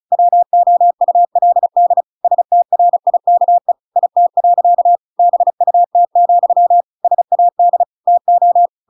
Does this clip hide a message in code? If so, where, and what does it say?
Yes: Morse code from 0.1 to 8.7 s, heard as a keyed tone, about 34 wpm, 700 Hz. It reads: WOULD STRIKE IT. BUT, SAD TO